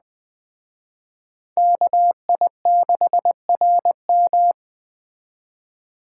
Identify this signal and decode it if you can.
Morse code, heard as a keyed tone, about 20 wpm, 700 Hz. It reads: KI6RM
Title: Speed Racing - USA 2x2 Call Signs 17wpm